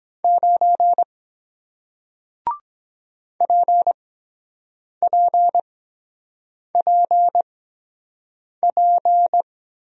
Morse code, heard as a keyed tone, about 23 wpm, spaced as for 7 wpm, 700 Hz. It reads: 9 PPPP